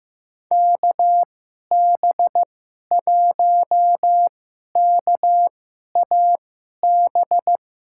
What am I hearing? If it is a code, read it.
Morse code, heard as a keyed tone, about 15 wpm, spaced as for 11 wpm, 700 Hz. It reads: KB1KAB
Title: USA Call Signs - 6 Characters - 2x Character Spacing - 15wpm